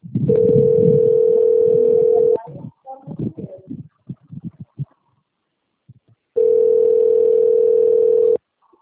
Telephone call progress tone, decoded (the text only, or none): ringback tone